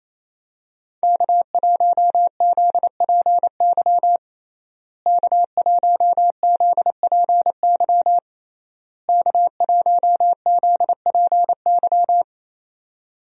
Morse code, heard as a keyed tone, about 27 wpm, 700 Hz. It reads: K1ZPY K1ZPY K1ZPY
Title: USA Call Signs - 5 Characters - Repeated 3x - 28wpm